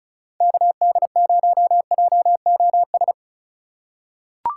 Morse code, heard as a keyed tone, about 36 wpm, 700 Hz. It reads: KD0JOS
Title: USA Call Signs - 6 Characters - 4x Character Spacing - 35wpm